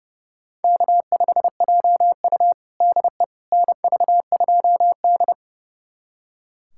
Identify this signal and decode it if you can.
Morse code, heard as a keyed tone, about 30 wpm, 700 Hz. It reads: K5JU DE NV2D
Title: USA Callsign DE Callsign - 4 Characters - 30wpm